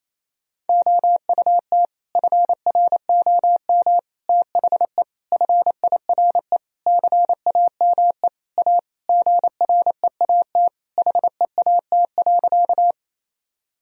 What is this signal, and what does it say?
Morse code, heard as a keyed tone, about 28 wpm, 700 Hz. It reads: OUT FROM THE FIRE CAME A GREAT HEAT.
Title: Sentences from Top 1000 Words - Review Part 3 - 28wpm